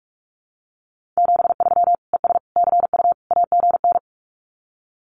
Morse code, heard as a keyed tone, about 44 wpm, 700 Hz. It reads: 73 ES CU AGN
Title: Common QSO Phrases Encore 45wpm